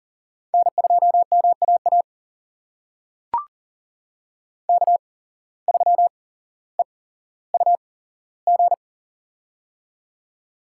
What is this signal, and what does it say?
Morse code, heard as a keyed tone, about 39 wpm, spaced as for 17 wpm, 700 Hz. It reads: N2MAA K3EUG
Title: USA Call Signs - 5 Characters - 8x Character Spacing - 40wpm